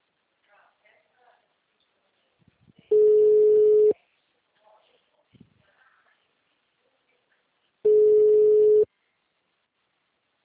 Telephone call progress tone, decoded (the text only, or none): ringback tone